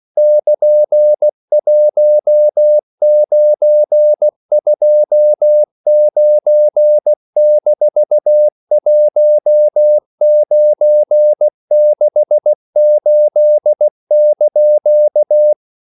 Morse code, heard as a keyed tone, about 16 wpm, 600 Hz. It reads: <KN>1929-1968)
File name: CW / Practice / EnglishQuotations0116